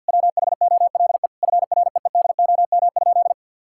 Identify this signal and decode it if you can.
Morse code, heard as a keyed tone, about 50 wpm, 700 Hz. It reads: WHOLE FREEDOM?
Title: Sentences from Top 1000 Words - Review Part 5 - 50wpm